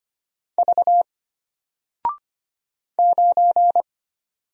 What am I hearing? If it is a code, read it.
Morse code, heard as a keyed tone, about 25 wpm, 700 Hz. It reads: V 9
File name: Single-Letter-Number-25wpm